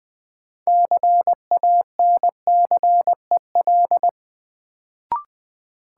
Morse code, heard as a keyed tone, about 20 wpm, 700 Hz. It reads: CANCEL
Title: Sentences from Top 1200 Words - New Word Drill - 20wpm